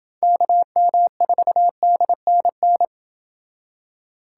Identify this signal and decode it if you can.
Morse code, heard as a keyed tone, about 28 wpm, 700 Hz. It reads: KM4DNN